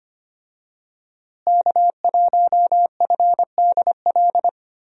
Morse code, heard as a keyed tone, about 25 wpm, 700 Hz. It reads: K1FDL
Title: USA Call Signs - 5 Characters - Repeated 2x - 25wpm